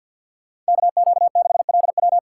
Morse code, heard as a keyed tone, about 65 wpm, 700 Hz. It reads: KQ6LW